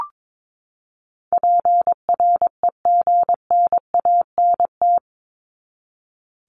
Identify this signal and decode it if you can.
Morse code, heard as a keyed tone, about 22 wpm, 700 Hz. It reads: PREGNANT
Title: Top 751-1000 Words 22wpm